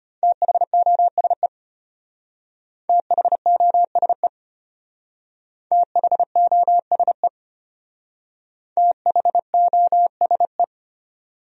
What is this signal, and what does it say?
Morse code, heard as a keyed tone, about 31 wpm, 700 Hz. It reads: THOSE THOSE THOSE THOSE